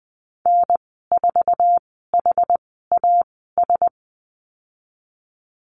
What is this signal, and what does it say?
Morse code, heard as a keyed tone, about 20 wpm, spaced as for 14 wpm, 700 Hz. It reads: N4HAS
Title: 062 Lesson 11g - Letter h - Call Signs 20wpm